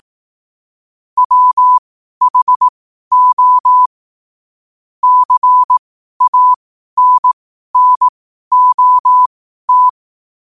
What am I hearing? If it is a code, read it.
Morse code, heard as a keyed tone, about 18 wpm, spaced as for 12 wpm, 1000 Hz. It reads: WHO CANNOT
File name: qotdfiles / 12wpm